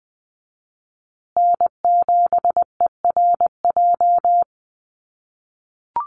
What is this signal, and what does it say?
Morse code, heard as a keyed tone, about 20 wpm, 700 Hz. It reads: N7ERJ
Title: USA Call Signs Repeated 2x 20wpm